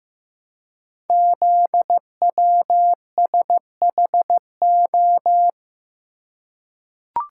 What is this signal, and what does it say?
Morse code, heard as a keyed tone, about 15 wpm, 700 Hz. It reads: ZWSHO
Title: Code Groups - 5 Letters - 15wpm